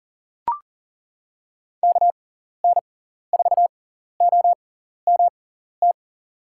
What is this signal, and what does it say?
Morse code, heard as a keyed tone, about 40 wpm, spaced as for 14 wpm, 700 Hz. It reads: KN4OMT
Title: USA Call Signs - 6 Characters - 6x Character Spacing - 40wpm